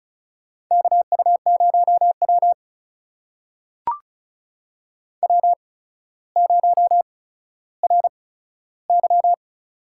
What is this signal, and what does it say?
Morse code, heard as a keyed tone, about 36 wpm, spaced as for 15 wpm, 700 Hz. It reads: KU0W W0RY
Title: USA Call Signs - 4 Characters - 8x Character Spacing - 35wpm